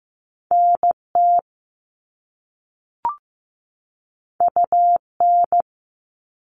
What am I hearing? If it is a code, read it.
Morse code, heard as a keyed tone, about 15 wpm, 700 Hz. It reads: NT UN
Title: All Cut Numbers - 2-Digit 15wpm